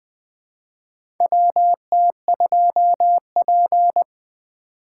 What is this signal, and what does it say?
Morse code, heard as a keyed tone, about 20 wpm, 700 Hz. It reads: W T 2 P
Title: USA Call Signs - 4 Characters - 6x Character Spacing - 20wpm